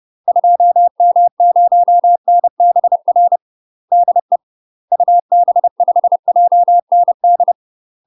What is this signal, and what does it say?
Morse code, heard as a keyed tone, about 30 wpm, 700 Hz. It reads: 2M0NBR DE UB5JND
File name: UK MASTER QSOs TRAINING / 30WPM / UK MASTER QSOs 0035